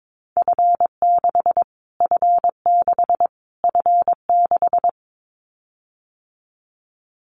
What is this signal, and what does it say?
Morse code, heard as a keyed tone, about 22 wpm, 700 Hz. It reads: F6 F6 F6
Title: Callsign Prefixes for Top 25 DXCC - Repeated 3x - 22wpm